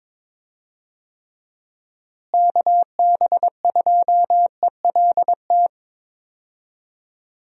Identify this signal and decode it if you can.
Morse code, heard as a keyed tone, about 22 wpm, 700 Hz. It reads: KB2ELT
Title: USA Call Signs - 6 Characters - Repeated 2x - 22wpm